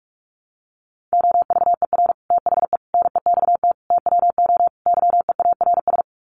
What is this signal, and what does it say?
Morse code, heard as a keyed tone, about 44 wpm, 700 Hz. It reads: OVER THE NEXT TWO YEARS
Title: Sets of 5 Words Encore2 45wpm